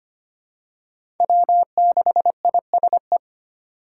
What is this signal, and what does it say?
Morse code, heard as a keyed tone, about 25 wpm, 700 Hz. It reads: W6ISE
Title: USA Call Signs - 5 Characters - 3x Character Spacing - 25wpm